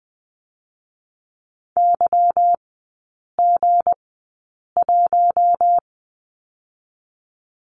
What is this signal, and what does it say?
Morse code, heard as a keyed tone, about 20 wpm, spaced as for 8 wpm, 690 Hz. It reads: YG1
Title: 159 Lesson 26e - Letter g - Three Characters 20wpm